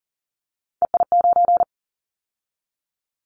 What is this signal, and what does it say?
Morse code, heard as a keyed tone, about 41 wpm, 700 Hz. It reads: EI9
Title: Callsign Prefixes for Top 50 DXCC - 40wpm